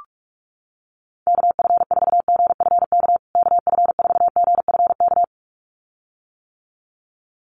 Morse code, heard as a keyed tone, about 44 wpm, 700 Hz. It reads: KF4GFK KF4GFK
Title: USA Call Signs Repeated 2x 45wpm